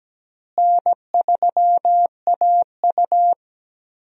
Morse code, heard as a keyed tone, about 17 wpm, 700 Hz. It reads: N3AU
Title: USA Call Signs - 4 Characters - Repeated 3x - 17wpm